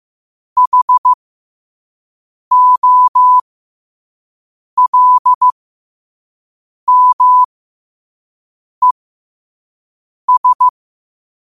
Morse code, heard as a keyed tone, about 15 wpm, spaced as for 5 wpm, 1000 Hz. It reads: HOLMES
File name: qotdfiles / 5wpm